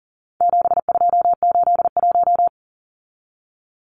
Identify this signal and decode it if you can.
Morse code, heard as a keyed tone, about 40 wpm, 700 Hz. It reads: 7281
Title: Numbers - 4-Digit 40wpm